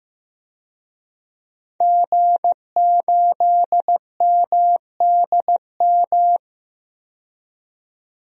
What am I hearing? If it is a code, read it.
Morse code, heard as a keyed tone, about 15 wpm, 700 Hz. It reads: G8MDM
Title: DXCC-100 Call Signs - 5 Characters - 15wpm